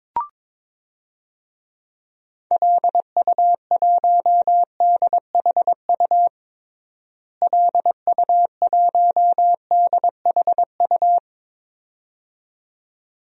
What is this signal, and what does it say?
Morse code, heard as a keyed tone, about 22 wpm, 700 Hz. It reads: LU1DHU LU1DHU
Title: DXCC-100 Call Signs - 6 Characters - Repeated 2x - 22wpm